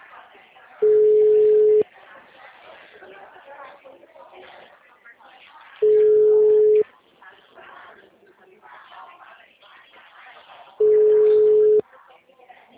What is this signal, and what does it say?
Ringback tone (Europe, 425 Hz), with speech.
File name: out- -2016- - - .53029